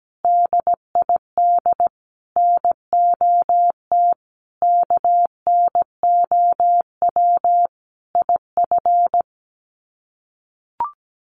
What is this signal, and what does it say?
Morse code, heard as a keyed tone, about 17 wpm, 700 Hz. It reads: DID NOT KNOW IF
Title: Sets of 5 Words Encore 17wpm